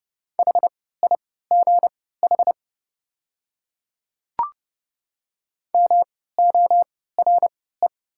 Morse code, heard as a keyed tone, about 29 wpm, spaced as for 17 wpm, 690 Hz. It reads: HIGH MORE
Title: 3x Character Spacing - Top 300 Words - 30wpm